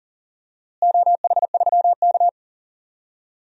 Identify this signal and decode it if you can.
Morse code, heard as a keyed tone, about 37 wpm, 700 Hz. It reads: OH3K
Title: DXCC-100 Call Signs - 4 Characters - Repeated 3x - 40wpm